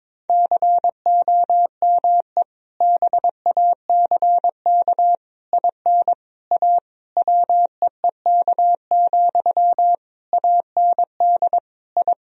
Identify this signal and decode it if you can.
Morse code, heard as a keyed tone, about 22 wpm, 700 Hz. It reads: COME BACK IN A WEEK, AND I
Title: Sentences from Top 1000 Words - Review Part 3 - 22wpm